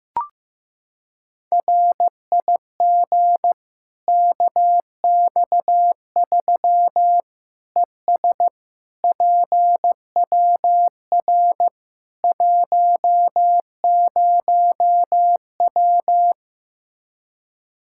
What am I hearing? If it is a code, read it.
Morse code, heard as a keyed tone, about 16 wpm, 700 Hz. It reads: RIG KX3 ES PWR 10W
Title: Practice QSO - Exchange 2 - Info - 15wpm